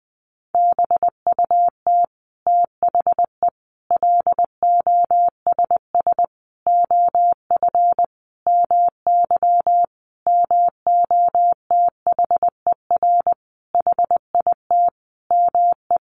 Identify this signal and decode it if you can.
Morse code, heard as a keyed tone, about 20 wpm, 700 Hz. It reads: BUT THE LOSS OF MY MOTHER HIT ME
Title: Sentences from Top 1400 Words 20wpm